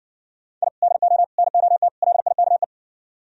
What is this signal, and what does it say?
Morse code, heard as a keyed tone, about 60 wpm, 700 Hz. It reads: I DO NOT LIKE